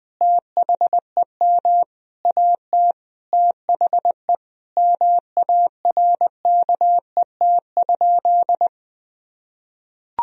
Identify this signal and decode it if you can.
Morse code, heard as a keyed tone, about 20 wpm, 700 Hz. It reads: THEM AT THE MARKET?
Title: Sentences from Top 1000 Words - Review Part 3 - 20wpm